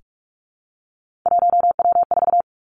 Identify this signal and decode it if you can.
Morse code, heard as a keyed tone, about 46 wpm, 700 Hz. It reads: 1W4
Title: Three Challenging Letter-Number Combinations 45wpm